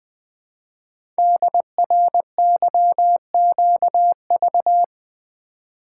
Morse code, heard as a keyed tone, about 20 wpm, 700 Hz. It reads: DRYQV
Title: Code Groups - 5 Letters - 20wpm